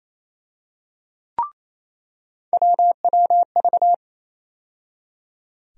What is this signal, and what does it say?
Morse code, heard as a keyed tone, about 28 wpm, 700 Hz. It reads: WWV
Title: USA Call Sign Postfixes - 3 Letters 28wpm